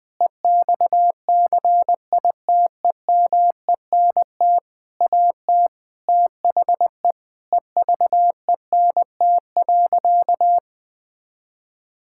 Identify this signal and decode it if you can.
Morse code, heard as a keyed tone, about 21 wpm, 700 Hz. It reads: EXCITEMENT AT THE EVENT.
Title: Sentences with 7-Word Phrases - 20wpm